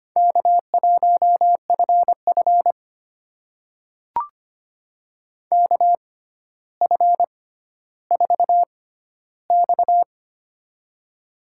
Morse code, heard as a keyed tone, about 25 wpm, spaced as for 13 wpm, 700 Hz. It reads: K1FF KF4X
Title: USA Call Signs - 4 Characters - 6x Character Spacing - 25wpm